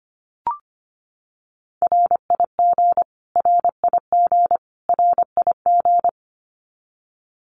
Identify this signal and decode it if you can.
Morse code, heard as a keyed tone, about 25 wpm, 700 Hz. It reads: RIG RIG RIG
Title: Abbreviations Repeated 3x - 25wpm